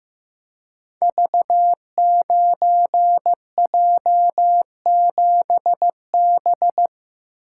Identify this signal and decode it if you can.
Morse code, heard as a keyed tone, about 15 wpm, 700 Hz. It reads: V9J7B
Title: Code Groups - 5 Letter:Numbers - 15wpm